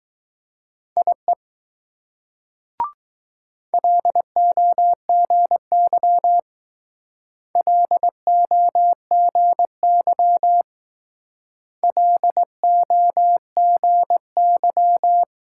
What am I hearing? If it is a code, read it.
Morse code, heard as a keyed tone, about 19 wpm, 700 Hz. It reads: IE LOGY LOGY LOGY